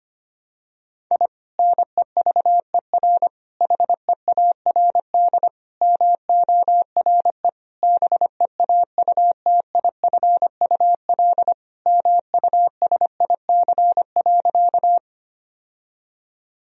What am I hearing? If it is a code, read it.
Morse code, heard as a keyed tone, about 25 wpm, 700 Hz. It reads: I NEVER HEARD MORE BEAUTIFUL MUSIC.
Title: Sentences from Top 1000 Words - Review Part 1 - 25wpm